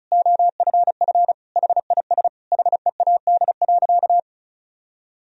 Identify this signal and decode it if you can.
Morse code, heard as a keyed tone, about 34 wpm, 700 Hz. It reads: OFF HIS HEAD.
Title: Sentences from Top 1000 Words - Review Part 1 - 35wpm